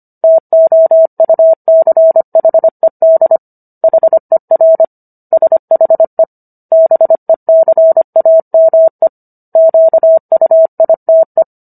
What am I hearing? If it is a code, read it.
Morse code, heard as a keyed tone, about 25 wpm, 650 Hz. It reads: TOUCHED HER SHE BECAME QUITE